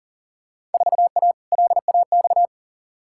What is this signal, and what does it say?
Morse code, heard as a keyed tone, about 40 wpm, 700 Hz. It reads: 4A LAX
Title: Field Day Exchanges - 40wpm